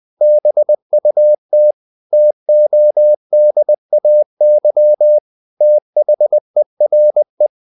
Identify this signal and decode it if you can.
Morse code, heard as a keyed tone, about 20 wpm, 600 Hz. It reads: BUT TODAY THERE
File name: the old man and the sea / the old man and the sea part 1 20wpm